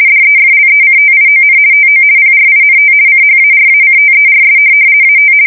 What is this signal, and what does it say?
RTTY, about 50 baud. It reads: DEFLECTION TO PRINT CHARACTERS AT
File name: fsk170-50B / teletype-corp